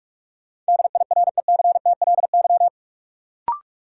Morse code, heard as a keyed tone, about 42 wpm, 700 Hz. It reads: DIRECTLY
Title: Sentences from Top 800 Words - New Word Drill - 45wpm